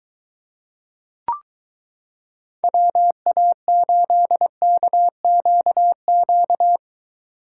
Morse code, heard as a keyed tone, about 22 wpm, 700 Hz. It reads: WA8KQQ